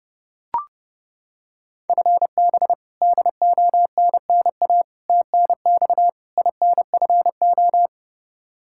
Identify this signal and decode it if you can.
Morse code, heard as a keyed tone, about 30 wpm, 700 Hz. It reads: FB DONNA TNX INFO
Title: Practice QSO - Exchange 3 - More Info - 30wpm